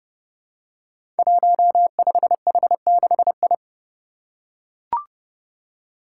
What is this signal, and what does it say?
Morse code, heard as a keyed tone, about 32 wpm, 700 Hz. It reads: 15H6I
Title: Five Challenging Letter-Number Combinations 30wpm